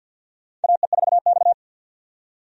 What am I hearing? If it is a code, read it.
Morse code, heard as a keyed tone, about 47 wpm, 700 Hz. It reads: AE4X
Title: USA Call Signs - 4 Characters - 50wpm